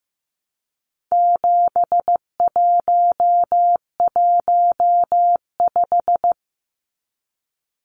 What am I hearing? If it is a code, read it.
Morse code, heard as a keyed tone, about 15 wpm, 700 Hz. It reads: 7115
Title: Numbers - 4-Digit 15wpm